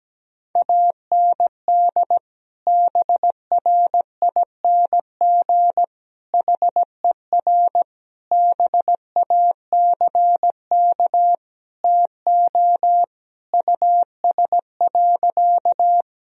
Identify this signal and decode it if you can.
Morse code, heard as a keyed tone, about 17 wpm, 700 Hz. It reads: AND BRING HER BACK TO US.
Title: Sentences from Top 1000 Words - Review Part 3 - 17wpm